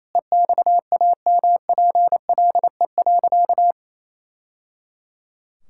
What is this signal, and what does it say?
Morse code, heard as a keyed tone, about 28 wpm, 700 Hz. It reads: EXAMPLE.
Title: Sentences from Top 1000 Words - Review Part 6 - 28wpm